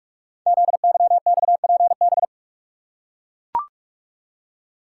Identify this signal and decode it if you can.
Morse code, heard as a keyed tone, about 44 wpm, 700 Hz. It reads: ZYXPB
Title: Code Groups - 5 Letter:Numbers - 45wpm